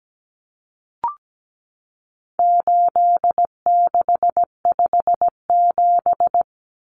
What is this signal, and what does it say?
Morse code, heard as a keyed tone, about 17 wpm, 700 Hz. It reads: 8657
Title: Common Cut Numbers - 4-Digit 17wpm